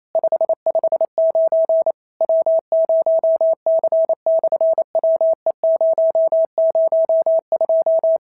Morse code, heard as a keyed tone, about 28 wpm, 650 Hz. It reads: 559 W0C/WE002
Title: SOTA QSO Practice 28wpm